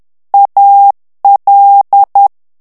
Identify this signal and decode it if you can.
Morse code, heard as a keyed tone, about 10 wpm, 800 Hz. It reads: AL
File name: ITALIEN texte clair 2 vitesse 10 mots minute